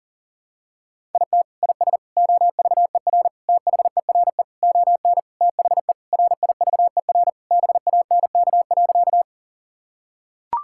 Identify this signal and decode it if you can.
Morse code, heard as a keyed tone, about 40 wpm, 700 Hz. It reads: IT IS OVER THERE ON THE RIVER BANK.
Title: Sentences from Top 1000 Words - Review Part 6 - 40wpm